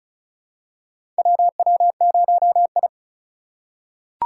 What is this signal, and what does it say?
Morse code, heard as a keyed tone, about 34 wpm, 700 Hz. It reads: WW0I
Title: USA Call Signs - 4 Characters - 2x Character Spacing - 35wpm